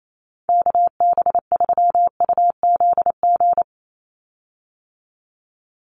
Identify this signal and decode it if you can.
Morse code, heard as a keyed tone, about 28 wpm, 700 Hz. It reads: KB3UZG
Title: USA Call Signs 28wpm